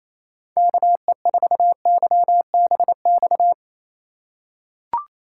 Morse code, heard as a keyed tone, about 28 wpm, 700 Hz. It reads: KE4YBX
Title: USA Call Signs - 6 Characters - 3x Character Spacing - 28wpm